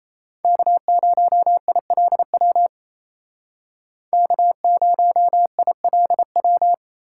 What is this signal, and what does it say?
Morse code, heard as a keyed tone, about 32 wpm, 700 Hz. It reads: K0ILW K0ILW
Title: Speed Racing - USA Call Signs 28wpm